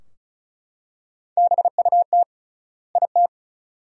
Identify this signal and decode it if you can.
Morse code, heard as a keyed tone, about 36 wpm, 700 Hz. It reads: BUT IT
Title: 3x Word-Spacing - Sentences from Top 100 Words - 35wpm